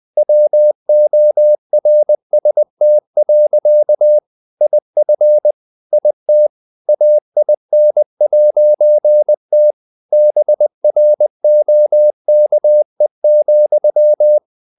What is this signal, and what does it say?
Morse code, heard as a keyed tone, about 20 wpm, 600 Hz. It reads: WORST. IF IT AIN'T BROKE,
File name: CW / Practice / EnglishSayings20